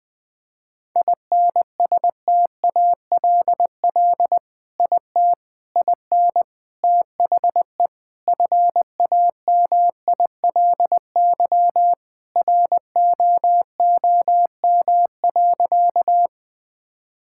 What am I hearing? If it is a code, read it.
Morse code, heard as a keyed tone, about 21 wpm, 700 Hz. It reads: INSTALL IT IN THE FAMILY ROOM.
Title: Sentences from Top 1600 Words 20wpm